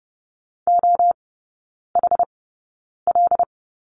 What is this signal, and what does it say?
Morse code, heard as a keyed tone, about 30 wpm, spaced as for 9 wpm, 700 Hz. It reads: OHL
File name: 081 Lesson 14e - Number 2 - Three Characters 30wpm